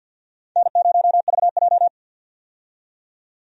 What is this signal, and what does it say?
Morse code, heard as a keyed tone, about 51 wpm, 700 Hz. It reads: N0VJ
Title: USA Call Signs - 4 Characters - 50wpm